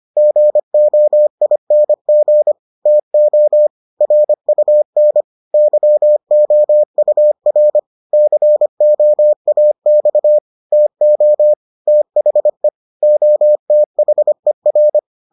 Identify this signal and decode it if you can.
Morse code, heard as a keyed tone, about 25 wpm, 600 Hz. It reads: GOING TO RUN YOUR COAX TO THE OTHER